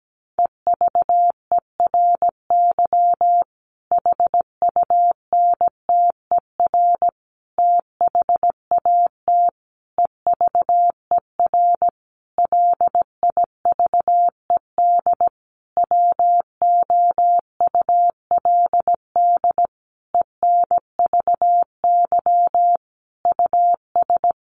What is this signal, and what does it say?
Morse code, heard as a keyed tone, about 17 wpm, 700 Hz. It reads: EVERY HUNTER THAT EVER LIVED WOULD ENVY US
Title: A Sound Of Thunder 17wpm